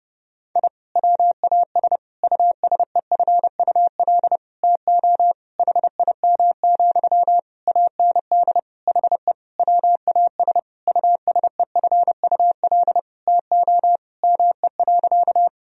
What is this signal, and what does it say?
Morse code, heard as a keyed tone, about 32 wpm, 700 Hz. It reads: I WAS USEFUL TO HIM, AND HE WAS USEFUL TO ME.
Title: Sentences from Top 1000 Words - Review Part 4 - 30wpm